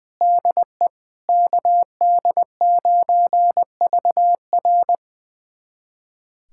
Morse code, heard as a keyed tone, about 20 wpm, 700 Hz. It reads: DE KD9VR
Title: USA Callsign DE Callsign - 5 Characters - 20wpm